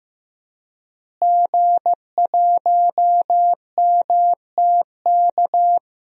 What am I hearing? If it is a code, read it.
Morse code, heard as a keyed tone, about 15 wpm, 700 Hz. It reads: G1MTK
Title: DXCC-100 Call Signs - 5 Characters - Repeated 2x - 15wpm